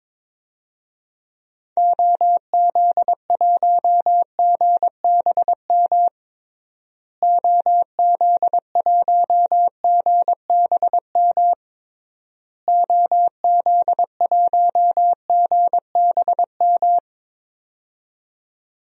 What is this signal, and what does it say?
Morse code, heard as a keyed tone, about 22 wpm, 700 Hz. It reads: OZ1GBM OZ1GBM OZ1GBM
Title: DXCC-100 Call Signs - 6 Characters - Repeated 3x - 22wpm